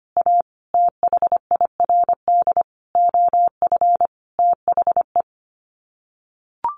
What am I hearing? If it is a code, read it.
Morse code, heard as a keyed tone, about 25 wpm, 700 Hz. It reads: A THIRD OF THE
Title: Sets of 5 Words Encore2 25wpm